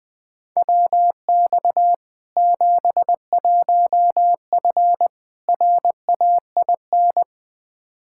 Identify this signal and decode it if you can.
Morse code, heard as a keyed tone, about 20 wpm, 700 Hz. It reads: WX 71F RAIN
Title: K1USN SST QSO Practice 20wpm